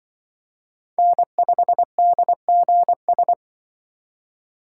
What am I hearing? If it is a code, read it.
Morse code, heard as a keyed tone, about 23 wpm, 700 Hz. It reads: N5DGS
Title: Speed Racing - USA 1x3 Call Signs 20wpm